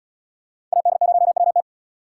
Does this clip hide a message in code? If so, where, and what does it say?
Yes: Morse code from 0.7 to 1.6 s, heard as a keyed tone, about 77 wpm, 700 Hz. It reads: AD0WT